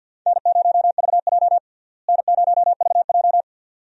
Morse code, heard as a keyed tone, about 52 wpm, 700 Hz. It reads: N0VJ N0VJ
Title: USA Call Signs - 4 Characters - Repeated 2x - 50wpm